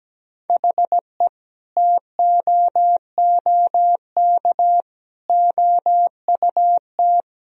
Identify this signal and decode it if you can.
Morse code, heard as a keyed tone, about 17 wpm, 700 Hz. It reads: HE TOOK OUT
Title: Sentences from Top 1000 Words - Review Part 4 - 17wpm